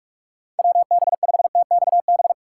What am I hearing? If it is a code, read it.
Morse code, heard as a keyed tone, about 45 wpm, 700 Hz. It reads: WB5TXB